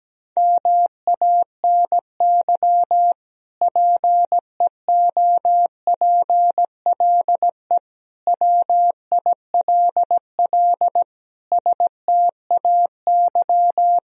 Morse code, heard as a keyed tone, about 17 wpm, 700 Hz. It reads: MANY PEOPLE WILL STAY
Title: Sentences from Top 1000 Words - Review Part 1 - 17wpm